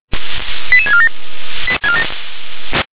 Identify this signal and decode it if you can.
Five-tone selective call (ZVEI), heard as quick runs of five tones, tones of about 80 ms. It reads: 88647
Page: ZVEI